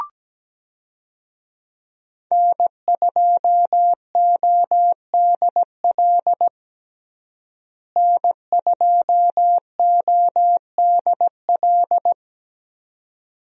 Morse code, heard as a keyed tone, about 17 wpm, 700 Hz. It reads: N2ODL N2ODL
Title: USA Call Signs - 5 Characters - Repeated 2x - 17wpm